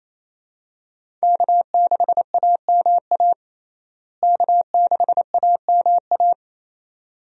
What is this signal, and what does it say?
Morse code, heard as a keyed tone, about 27 wpm, 700 Hz. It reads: K6AMA K6AMA
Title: USA Call Signs - 5 Characters - Repeated 2x - 28wpm